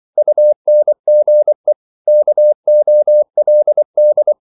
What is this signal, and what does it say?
Morse code, heard as a keyed tone, about 24 wpm, 600 Hz. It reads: UNGE KOLD
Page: Korte danske ord 24wpm